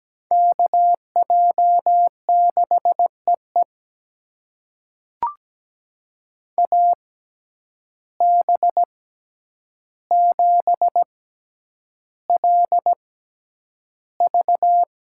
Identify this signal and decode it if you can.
Morse code, heard as a keyed tone, about 17 wpm, spaced as for 9 wpm, 700 Hz. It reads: KJ6EE AB7LV
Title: USA Call Signs - 5 Characters - 6x Character Spacing - 17wpm